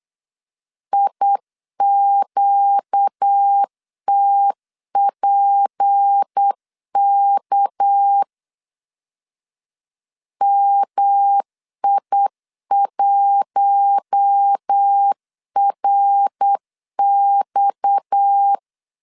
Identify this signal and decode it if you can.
Morse code, heard as a keyed tone, about 8 wpm, 800 Hz. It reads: IQTPK MI1RX
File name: 8wpm